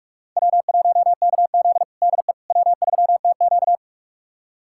Morse code, heard as a keyed tone, about 46 wpm, 700 Hz. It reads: W1KZ DE W3TQ
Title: USA Callsign DE Callsign - 4 Characters - 45wpm